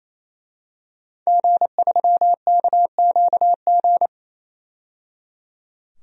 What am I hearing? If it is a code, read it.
Morse code, heard as a keyed tone, about 28 wpm, 700 Hz. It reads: G3KQG
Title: DXCC-100 Call Signs - 5 Characters - 28wpm